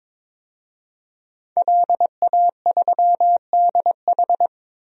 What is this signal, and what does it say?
Morse code, heard as a keyed tone, about 22 wpm, 700 Hz. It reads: LA3DH
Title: DXCC-100 Call Signs - 5 Characters - Repeated 2x - 22wpm